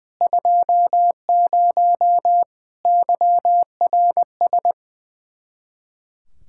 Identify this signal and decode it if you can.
Morse code, heard as a keyed tone, about 21 wpm, 700 Hz. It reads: 20 YRS
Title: K1USN SST QSO Practice 20wpm